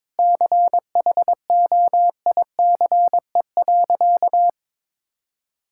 Morse code, heard as a keyed tone, about 22 wpm, 700 Hz. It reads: CHOICE.
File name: practice / 22wpm / Sentences with 4-Word Phrases - 22wpm